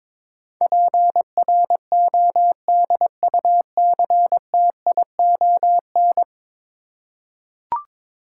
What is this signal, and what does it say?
Morse code, heard as a keyed tone, about 22 wpm, 700 Hz. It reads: PRODUCTION
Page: morse-code-ninja-practice-22wpm